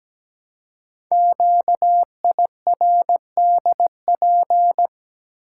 Morse code, heard as a keyed tone, about 17 wpm, 700 Hz. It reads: QIRDP
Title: Code Groups - 5 Letter:Numbers - 17wpm